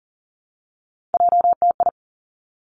Morse code, heard as a keyed tone, about 38 wpm, 700 Hz. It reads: JTI
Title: Three Challenging Letter-Number Combinations 40wpm